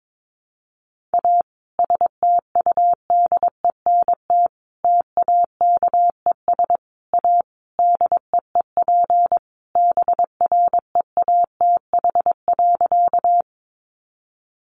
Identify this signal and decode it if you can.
Morse code, heard as a keyed tone, about 22 wpm, 700 Hz. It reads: A STUDENT TAKES A DEEP BREATH.
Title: Additional Easy Phrases 22wpm